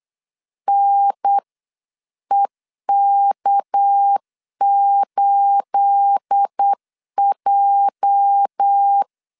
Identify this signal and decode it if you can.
Morse code, heard as a keyed tone, about 8 wpm, 800 Hz. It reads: NEK8J
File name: CW / DET / 1 9 1500C S B 8wpm 5624